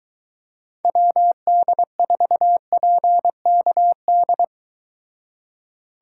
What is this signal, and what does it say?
Morse code, heard as a keyed tone, about 23 wpm, 700 Hz. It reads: WD4PKD